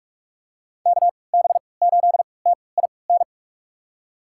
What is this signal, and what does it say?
Morse code, heard as a keyed tone, about 43 wpm, spaced as for 25 wpm, 700 Hz. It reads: KB8TIN
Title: USA Call Signs - 6 Characters - 3x Character Spacing - 45wpm